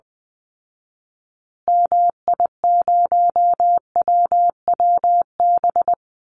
Morse code, heard as a keyed tone, about 20 wpm, 700 Hz. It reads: MI0WWB
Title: Call Signs Repeated 2x - 20wpm